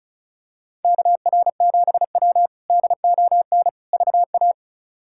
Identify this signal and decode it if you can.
Morse code, heard as a keyed tone, about 36 wpm, 700 Hz. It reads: KR7W DON VA
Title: K1USN SST QSO Practice - 35wpm